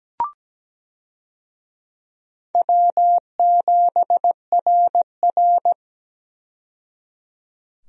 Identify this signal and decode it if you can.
Morse code, heard as a keyed tone, about 17 wpm, 700 Hz. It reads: W7RR
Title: USA Call Signs - 4 Characters - 17wpm